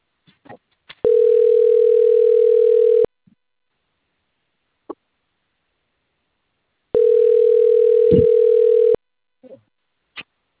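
Ringback tone (North America), 440 + 480 Hz.
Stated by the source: Outgoing telephone call recorded on a system in Limeira.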